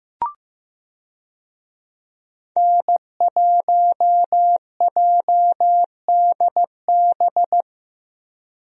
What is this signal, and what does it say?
Morse code, heard as a keyed tone, about 15 wpm, 700 Hz. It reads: N1JDB
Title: USA Call Signs - 5 Characters - Repeated 2x - 15wpm